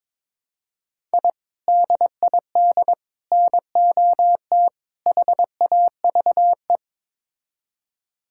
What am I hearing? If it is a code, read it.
Morse code, heard as a keyed tone, about 22 wpm, 700 Hz. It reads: I DID NOT HAVE
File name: practice / 22wpm / Build A Phrase - 5-Word Phrases - 22wpm